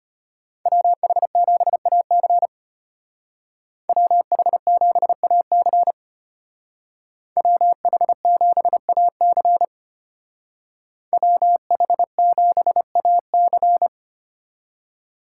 Morse code, heard as a keyed tone, about 30 wpm, 700 Hz. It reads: WH7AC WH7AC WH7AC WH7AC
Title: Speed Racing - USA Call Signs 25wpm